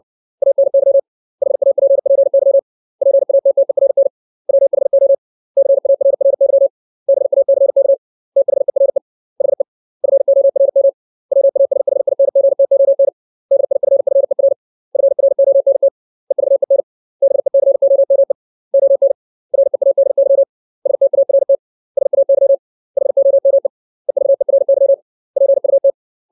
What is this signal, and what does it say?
Morse code, heard as a keyed tone, about 60 wpm, 550 Hz. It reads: ANY HAPPY PATTERN WHO CARRY BACK THERE HE FORM WASHINGTON DIFFER FRONT EVEN BECOME ON READY START SAY HOME EVERY PUT
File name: 632 words / 632 Words 60WPM